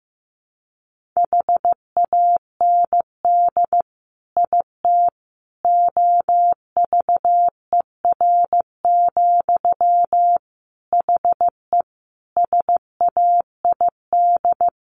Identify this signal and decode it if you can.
Morse code, heard as a keyed tone, about 15 wpm, 700 Hz. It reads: HAND IT OVER, HE SAID
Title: Sentences from Top 1500 Words 15wpm